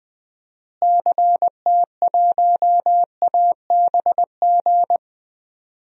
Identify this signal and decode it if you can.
Morse code, heard as a keyed tone, about 21 wpm, 700 Hz. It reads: CT1ABG
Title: DXCC-100 Call Signs - 6 Characters - Repeated 2x - 20wpm